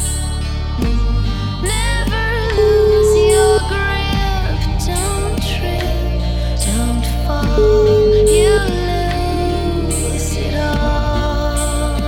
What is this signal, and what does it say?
Ringback tone (Europe, 425 Hz), with noise.